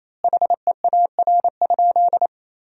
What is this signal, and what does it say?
Morse code, heard as a keyed tone, about 28 wpm, 700 Hz. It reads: HEAR?
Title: 4x Word-Spacing - Sentences from Top 300 Words - 28wpm